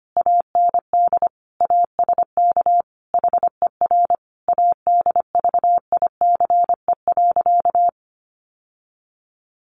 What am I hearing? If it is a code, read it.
Morse code, heard as a keyed tone, about 25 wpm, 700 Hz. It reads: AND ASK HER ADVICE.
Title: Sentences from Top 1100 Words 25wpm